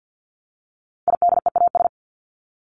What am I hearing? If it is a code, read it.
Morse code, heard as a keyed tone, about 48 wpm, 700 Hz. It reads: IDEAS
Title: ES:S Suffix - Top 100 Words with Base Words 50wpm